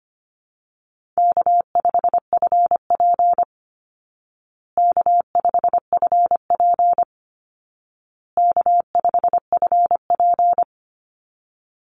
Morse code, heard as a keyed tone, about 25 wpm, 700 Hz. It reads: K5FP K5FP K5FP
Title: Easy Call Signs Repeated 3x - 25wpm